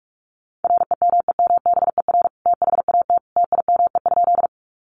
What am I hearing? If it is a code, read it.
Morse code, heard as a keyed tone, about 44 wpm, 700 Hz. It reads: REMEMBER THAT TIME?
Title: Sentences from Top 400 Words 45wpm